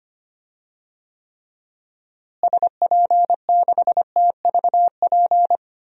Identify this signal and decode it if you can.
Morse code, heard as a keyed tone, about 25 wpm, 700 Hz. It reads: SP6TVP
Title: DXCC-100 Call Signs - 6 Characters - Repeated 2x - 25wpm